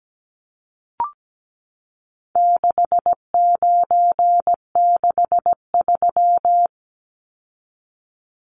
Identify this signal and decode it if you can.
Morse code, heard as a keyed tone, about 17 wpm, 700 Hz. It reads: 6963
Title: Numbers - 4-Digit 17wpm